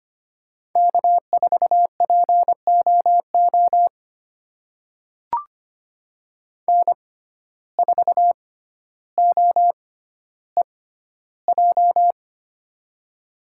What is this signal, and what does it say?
Morse code, heard as a keyed tone, about 25 wpm, spaced as for 13 wpm, 700 Hz. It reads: K4POO N4OEJ
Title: USA Call Signs - 5 Characters - 6x Character Spacing - 25wpm